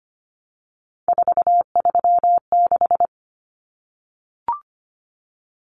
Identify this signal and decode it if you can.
Morse code, heard as a keyed tone, about 25 wpm, 700 Hz. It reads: 436
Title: Common Cut Numbers - 3-Digit 25wpm